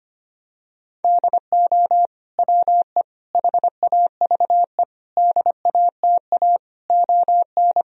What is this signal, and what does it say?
Morse code, heard as a keyed tone, about 25 wpm, 700 Hz. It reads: DO WE HAVE DATA ON
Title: Sentences from Top 1000 Words - Review Part 3 - 25wpm